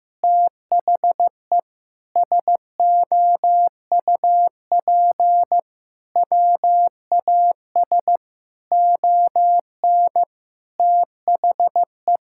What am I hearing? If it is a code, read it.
Morse code, heard as a keyed tone, about 15 wpm, 700 Hz. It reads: THE SOUP WAS ON THE
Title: Sentences from Top 1000 Words - Review Part 6 - 15wpm